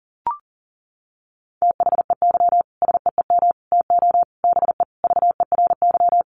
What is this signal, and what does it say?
Morse code, heard as a keyed tone, about 37 wpm, 700 Hz. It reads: THEY SEEM TO BE VERY
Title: Sentences from Top 1100 Words 40wpm